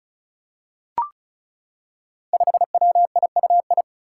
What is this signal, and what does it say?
Morse code, heard as a keyed tone, about 34 wpm, 700 Hz. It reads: 5WIUI
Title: Five Challenging Letter-Number Combinations 35wpm